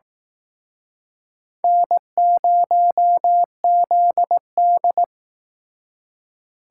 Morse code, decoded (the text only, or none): N0ZD